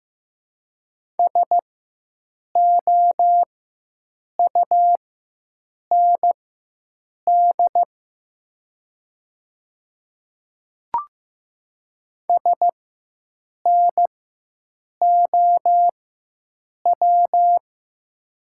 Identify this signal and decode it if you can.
Morse code, heard as a keyed tone, about 15 wpm, spaced as for 7 wpm, 700 Hz. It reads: SOUND SNOW
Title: 4x Character Spacing - Top 500 Words - 15wpm